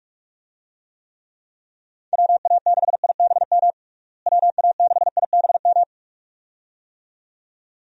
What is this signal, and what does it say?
Morse code, heard as a keyed tone, about 46 wpm, 700 Hz. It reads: WA6IBM WA6IBM
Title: USA Call Signs - 6 Characters - Repeated 2x - 45wpm